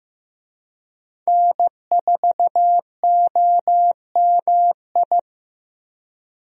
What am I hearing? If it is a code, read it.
Morse code, heard as a keyed tone, about 15 wpm, 700 Hz. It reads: N4OMI
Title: USA Call Signs - 5 Characters - Repeated 2x - 15wpm